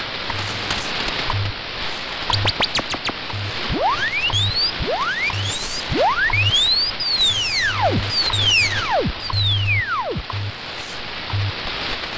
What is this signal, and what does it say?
Radio time signal, with noise.
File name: Radio / SDR Audio / WWV Atmospheric Propagation Experiment 5000kHz